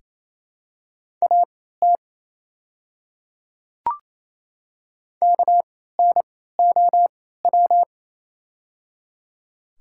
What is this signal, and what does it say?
Morse code, heard as a keyed tone, about 28 wpm, spaced as for 16 wpm, 700 Hz. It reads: AT KNOW
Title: 3x Character Spacing - Top 300 Words - 28wpm